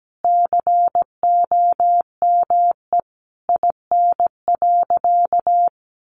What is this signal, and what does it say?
Morse code, heard as a keyed tone, about 17 wpm, 700 Hz. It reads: COME IN.
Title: Sentences from Top 1500 Words 17wpm